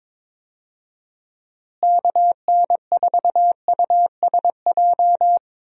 Morse code, heard as a keyed tone, about 22 wpm, 700 Hz. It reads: KN4USJ
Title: USA Call Signs - 6 Characters - Repeated 3x - 22wpm